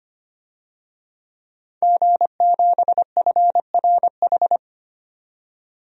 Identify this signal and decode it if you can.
Morse code, heard as a keyed tone, about 25 wpm, 700 Hz. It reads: G7FRH
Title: DXCC-100 Call Signs - 5 Characters - 25wpm